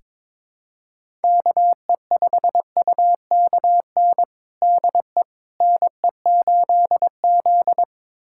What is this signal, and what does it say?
Morse code, heard as a keyed tone, about 22 wpm, 700 Hz. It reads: KE5UKN DE NE8Z
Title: USA Callsign DE Callsign 22wpm